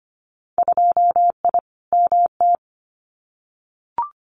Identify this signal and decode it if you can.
Morse code, heard as a keyed tone, about 25 wpm, 700 Hz. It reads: 2I MT
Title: Winter Field Day Exchanges Encore 25wpm